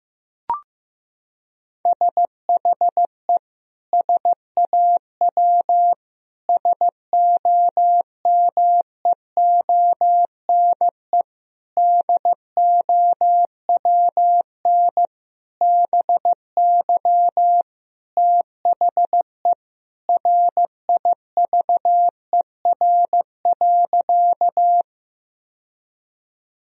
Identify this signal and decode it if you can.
Morse code, heard as a keyed tone, about 15 wpm, 700 Hz. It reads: SHE SAW SOMEONE DOWN BY THE RIVER.
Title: Sentences from Top 1000 Words - Review Part 4 - 15wpm